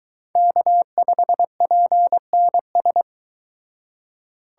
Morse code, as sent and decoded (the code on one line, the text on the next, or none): -.- ..... .--. -. ...
K5PNS